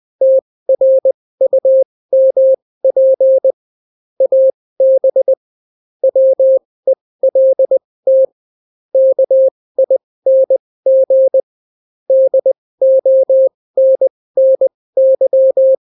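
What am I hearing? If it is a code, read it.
Morse code, heard as a keyed tone, about 20 wpm, spaced as for 16 wpm, 530 Hz. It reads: TRUMP AB WELT KING DONNY
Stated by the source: Daily Morse weather and top stories at 16 wpm, clean without QRN.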